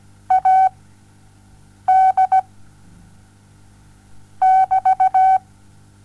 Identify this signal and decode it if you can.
Morse code, heard as a keyed tone, about 16 wpm, spaced as for 6 wpm, 760 Hz. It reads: AD <BT>
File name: Onlinelehrgang / morsen / DJ4UF Morsen07